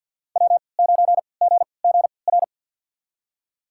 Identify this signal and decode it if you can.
Morse code, heard as a keyed tone, about 46 wpm, spaced as for 27 wpm, 700 Hz. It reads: W9GGR
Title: USA Call Signs - 5 Characters - 3x Character Spacing - 50wpm